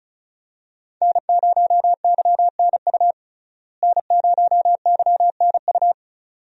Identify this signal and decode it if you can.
Morse code, heard as a keyed tone, about 36 wpm, 700 Hz. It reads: N0YNU N0YNU
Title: USA Call Signs - 5 Characters - Repeated 2x - 35wpm